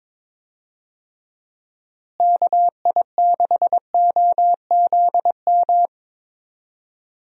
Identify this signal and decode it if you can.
Morse code, heard as a keyed tone, about 22 wpm, 700 Hz. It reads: KI6OZM
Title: USA Call Signs - 6 Characters - Repeated 3x - 22wpm